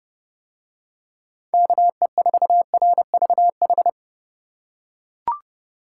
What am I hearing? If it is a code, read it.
Morse code, heard as a keyed tone, about 30 wpm, 700 Hz. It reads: KE4RVH